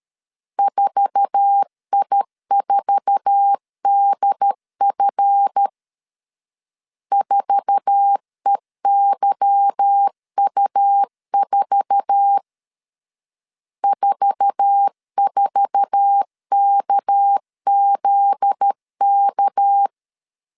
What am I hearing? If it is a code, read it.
Morse code, heard as a keyed tone, about 13 wpm, 800 Hz. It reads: 4I4DF 4EYU4 44KZK
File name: LEZIONI / 8 YJ45 / 8 YJ4 12wpm 2151